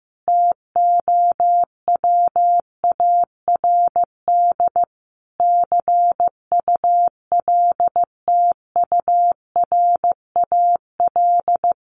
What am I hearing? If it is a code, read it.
Morse code, heard as a keyed tone, about 15 wpm, 700 Hz. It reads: TOWARD CULTURAL
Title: Sentences from Top 1500 Words 15wpm